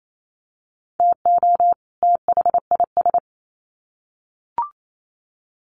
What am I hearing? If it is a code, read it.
Morse code, heard as a keyed tone, about 28 wpm, 700 Hz. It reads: TO THIS
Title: Sets of 2 Words 28wpm